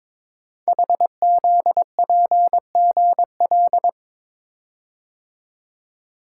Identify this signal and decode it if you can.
Morse code, heard as a keyed tone, about 23 wpm, 700 Hz. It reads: HZPGL
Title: Code Groups - 5 Letters - 22wpm